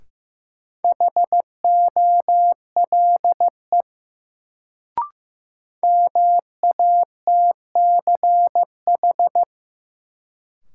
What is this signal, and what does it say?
Morse code, heard as a keyed tone, about 15 wpm, 700 Hz. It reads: HOLE MATCH
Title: Sentences from Top 1600 Words - New Word Drill - 15wpm